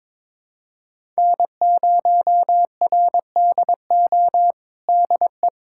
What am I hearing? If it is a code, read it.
Morse code, heard as a keyed tone, about 22 wpm, 700 Hz. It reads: N0RDO DE
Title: USA Callsign DE Callsign - 5 Characters - 22wpm